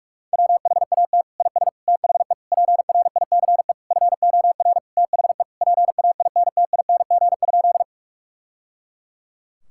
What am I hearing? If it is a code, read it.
Morse code, heard as a keyed tone, about 46 wpm, 700 Hz. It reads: WHAT IS THE PRICE FOR THE PAINTING?
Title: Sentences from Top 1000 Words - Review Part 6 - 45wpm